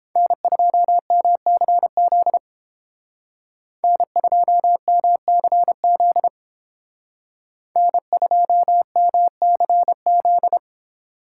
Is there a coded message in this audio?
Yes: Morse code from 0.2 to 10.6 s, heard as a keyed tone, about 29 wpm, 700 Hz. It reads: N2MCZ N2MCZ N2MCZ